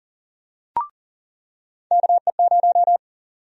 Morse code, heard as a keyed tone, about 39 wpm, 700 Hz. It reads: KE0
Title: USA Call Sign Prefixes - 40wpm